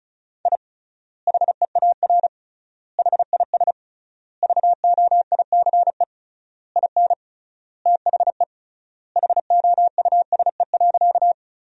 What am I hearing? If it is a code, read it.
Morse code, heard as a keyed tone, about 34 wpm, 700 Hz. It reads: I HEAR HIS VOICE IN THE HOUSE.
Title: 3x Word-Spacing - Sentences from Top 500 Words - 35wpm